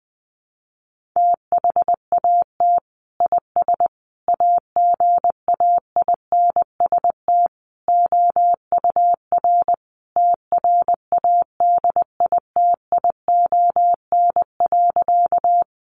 Morse code, heard as a keyed tone, about 20 wpm, 700 Hz. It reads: THAT IS AGAINST OUR TRADITION.
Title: Sentences from Top 1500 Words 20wpm